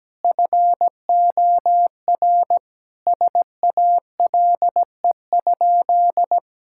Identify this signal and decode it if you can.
Morse code, heard as a keyed tone, about 17 wpm, 700 Hz. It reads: FOR SALE?
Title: Sentences from Top 1600 Words 17wpm